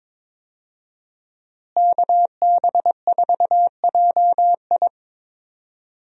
Morse code, heard as a keyed tone, about 22 wpm, 700 Hz. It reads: KB4JI
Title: USA Call Signs - 5 Characters - Repeated 2x - 22wpm